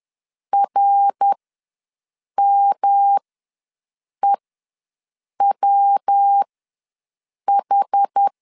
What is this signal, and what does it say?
Morse code, heard as a keyed tone, about 10 wpm, spaced as for 6 wpm, 800 Hz. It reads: RMEWH
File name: LEZIONI / 6 PKX0 / 6 PK ws 10wpm 3745